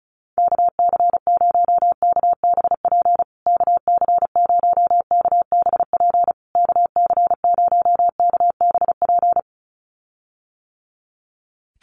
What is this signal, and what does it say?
Morse code, heard as a keyed tone, about 34 wpm, 700 Hz. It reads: KC0KBP KC0KBP KC0KBP
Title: USA Call Signs Repeated 3x 35wpm